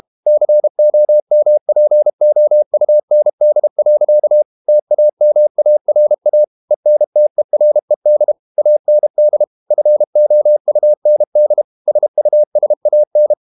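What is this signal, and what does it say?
Morse code, heard as a keyed tone, about 32 wpm, 600 Hz. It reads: COMPOUND. TAMARA ENTERED AND FOUND SUSAN
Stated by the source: Morse code practice at 32 wpm from CW med Gnister, reading Ken Follett's novel Never in English.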